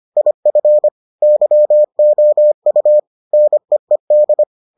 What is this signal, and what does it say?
Morse code, heard as a keyed tone, about 25 wpm, 600 Hz. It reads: IF YOU NEED